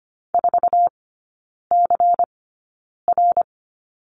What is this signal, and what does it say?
Morse code, heard as a keyed tone, about 25 wpm, spaced as for 9 wpm, 700 Hz. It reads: 4CR
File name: course / 25:10wpm / 093 Lesson 16e - Letter c - Three Characters 25wpm